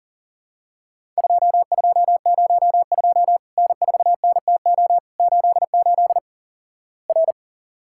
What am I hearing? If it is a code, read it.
Morse code, heard as a keyed tone, about 37 wpm, 700 Hz. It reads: 2202 N4NTO 88 R
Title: Worked All Europe DX Contest - QTC Exchange - 40wpm